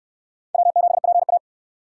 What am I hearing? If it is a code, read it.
Morse code, heard as a keyed tone, about 61 wpm, 700 Hz. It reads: W7PA